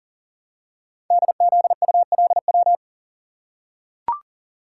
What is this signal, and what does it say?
Morse code, heard as a keyed tone, about 42 wpm, 700 Hz. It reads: DZULW
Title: Code Groups - 5 Letters - 40wpm